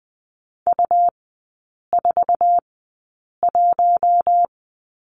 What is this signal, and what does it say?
Morse code, heard as a keyed tone, about 20 wpm, spaced as for 8 wpm, 700 Hz. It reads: U41
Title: 100 Lesson 17e - Letter u - Three Characters 20wpm